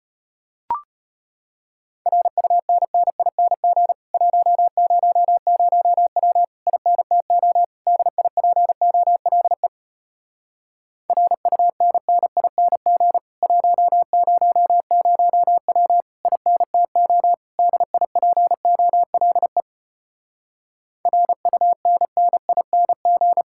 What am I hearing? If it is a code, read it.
Morse code, heard as a keyed tone, about 33 wpm, 700 Hz. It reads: RUNNING 100W INTO DIPOLE RUNNING 100W INTO DIPOLE RUNNING